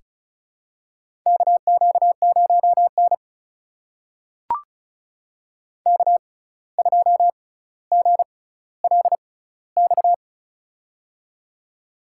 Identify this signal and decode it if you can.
Morse code, heard as a keyed tone, about 36 wpm, spaced as for 17 wpm, 700 Hz. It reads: KQ0N K2GLX
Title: USA Call Signs - 6x Character Spacing - 35wpm